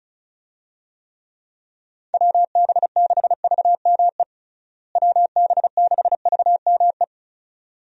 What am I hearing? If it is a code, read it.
Morse code, heard as a keyed tone, about 36 wpm, 700 Hz. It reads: WB6VME WB6VME
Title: USA Call Signs - 6 Characters - Repeated 2x - 35wpm